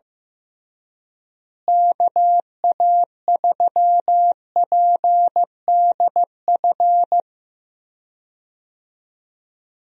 Morse code, heard as a keyed tone, about 15 wpm, 700 Hz. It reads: KA3PDF
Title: USA Call Signs - 6 Characters - Repeated 3x - 15wpm